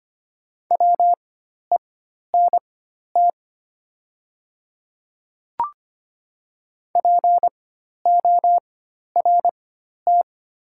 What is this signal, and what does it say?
Morse code, heard as a keyed tone, about 25 wpm, spaced as for 12 wpm, 700 Hz. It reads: WENT PORT
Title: 4x Character Spacing - Top 200 Words - 25wpm